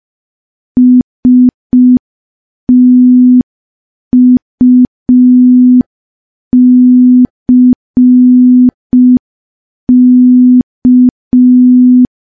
Morse code, heard as a keyed tone, about 5 wpm, 250 Hz. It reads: STUCK
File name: morse / morse-training-05WPM